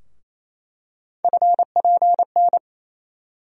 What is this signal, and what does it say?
Morse code, heard as a keyed tone, about 28 wpm, 700 Hz. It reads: FPN
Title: USA Call Sign Postfixes - 3 Letters 28wpm